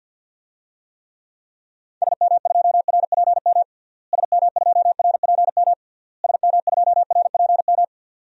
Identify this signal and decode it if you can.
Morse code, heard as a keyed tone, about 51 wpm, 700 Hz. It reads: SM2RPM SM2RPM SM2RPM
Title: DXCC-100 Call Signs - 6 Characters - Repeated 3x - 50wpm